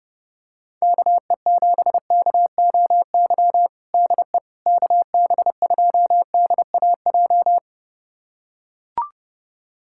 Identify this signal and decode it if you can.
Morse code, heard as a keyed tone, about 32 wpm, 700 Hz. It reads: KE7KOY DE KB2DAJ
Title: USA Callsign DE Callsign - 6 Characters - 30wpm